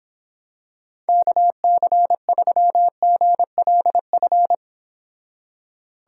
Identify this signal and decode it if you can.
Morse code, heard as a keyed tone, about 26 wpm, 700 Hz. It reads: KC3GLF